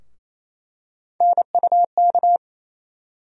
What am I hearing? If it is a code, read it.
Morse code, heard as a keyed tone, about 28 wpm, 700 Hz. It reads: NUK
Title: USA Call Sign Postfixes - 3 Letters 28wpm